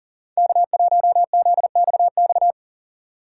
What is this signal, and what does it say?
Morse code, heard as a keyed tone, about 40 wpm, 700 Hz. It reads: K1ZXX
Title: USA Call Signs - 5 Characters - 4x Character Spacing - 40wpm